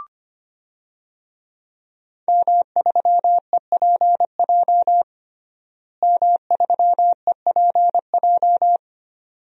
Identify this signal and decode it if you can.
Morse code, heard as a keyed tone, about 25 wpm, 700 Hz. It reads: M3EPJ M3EPJ
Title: DXCC-100 Call Signs - 5 Characters - Repeated 2x - 25wpm